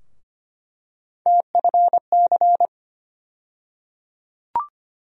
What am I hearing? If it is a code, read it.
Morse code, heard as a keyed tone, about 25 wpm, 700 Hz. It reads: TFC
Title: USA Call Sign Postfixes - 3 Letters 25wpm